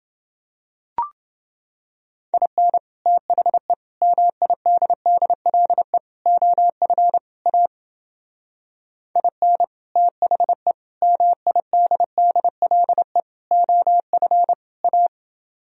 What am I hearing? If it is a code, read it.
Morse code, heard as a keyed tone, about 28 wpm, 700 Hz. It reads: IN THE MIDDLE OF A IN THE MIDDLE OF A